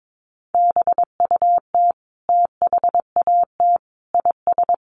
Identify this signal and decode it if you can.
Morse code, heard as a keyed tone, about 22 wpm, 700 Hz. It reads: BUT THAT IS
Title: Sentences from Top 1500 Words 22wpm